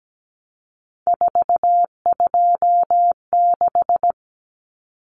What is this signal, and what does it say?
Morse code, heard as a keyed tone, about 17 wpm, 700 Hz. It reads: 426
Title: Common Cut Numbers - 3-Digit 17wpm